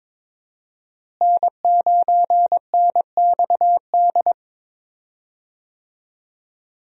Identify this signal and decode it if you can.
Morse code, heard as a keyed tone, about 22 wpm, 700 Hz. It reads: N9NXD
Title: USA Call Signs - 5 Characters - Repeated 2x - 22wpm